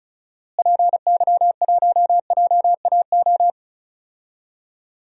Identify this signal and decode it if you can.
Morse code, heard as a keyed tone, about 36 wpm, 700 Hz. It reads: PY1JAO
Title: DXCC-100 Call Signs - 6 Characters - Repeated 2x - 35wpm